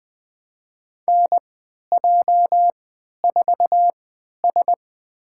Morse code, heard as a keyed tone, about 20 wpm, spaced as for 11 wpm, 700 Hz. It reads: NJ4S
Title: USA Call Signs - 4 Characters - 3x Character Spacing - 20wpm